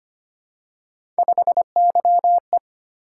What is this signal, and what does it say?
Morse code, heard as a keyed tone, about 25 wpm, 700 Hz. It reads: 5YE
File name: Three Letter-Number Combinations - 25wpm